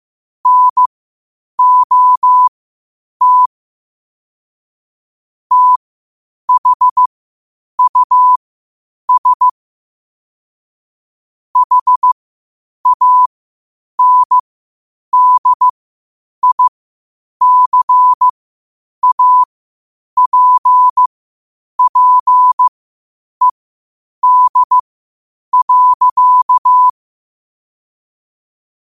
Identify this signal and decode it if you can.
Morse code, heard as a keyed tone, about 15 wpm, spaced as for 8 wpm, 1000 Hz. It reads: NOT THUS HANDICAPPED.
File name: qotdfiles / 8wpm